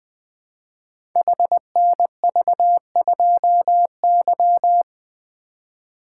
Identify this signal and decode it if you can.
Morse code, heard as a keyed tone, about 20 wpm, 700 Hz. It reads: HNV2Y
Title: Code Groups - 5 Letter:Numbers - 20wpm